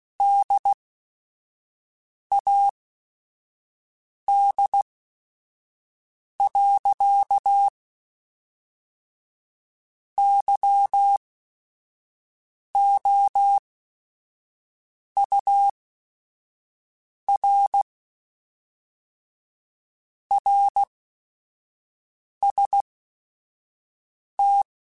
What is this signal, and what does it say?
Morse code, heard as a keyed tone, about 16 wpm, spaced as for 5 wpm, 800 Hz. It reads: DAD. YOUR RST
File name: QSOs / QSOs-mp3s / WD0BGP 2@5